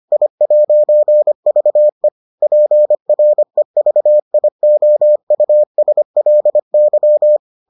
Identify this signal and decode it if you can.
Morse code, heard as a keyed tone, about 25 wpm, 600 Hz. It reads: I'VE PREVIOUSLY